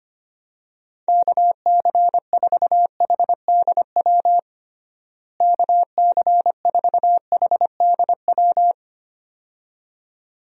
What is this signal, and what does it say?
Morse code, heard as a keyed tone, about 25 wpm, 700 Hz. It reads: KC4HDW KC4HDW
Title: USA Call Signs - 6 Characters - Repeated 2x - 25wpm